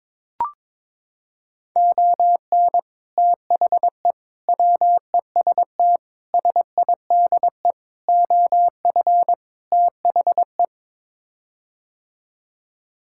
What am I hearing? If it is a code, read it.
Morse code, heard as a keyed tone, about 22 wpm, 700 Hz. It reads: ON THE WEST SIDE OF THE
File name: practice / 22wpm / Sets of 6 Words - 22wpm